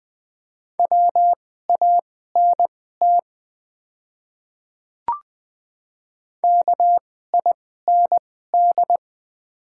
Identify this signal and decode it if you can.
Morse code, heard as a keyed tone, about 20 wpm, spaced as for 14 wpm, 700 Hz. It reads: WANT KIND
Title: 2x Character Spacing - Top 300 Words - 20wpm